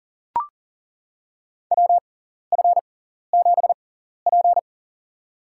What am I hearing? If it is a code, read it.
Morse code, heard as a keyed tone, about 40 wpm, spaced as for 14 wpm, 700 Hz. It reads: WF7P
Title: USA Call Signs - 4 Characters - 6x Character Spacing - 40wpm